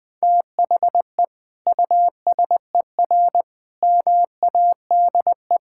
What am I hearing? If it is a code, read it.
Morse code, heard as a keyed tone, about 20 wpm, 700 Hz. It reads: THE USER MADE
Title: Sentences from Top 1000 Words - Review Part 5 - 20wpm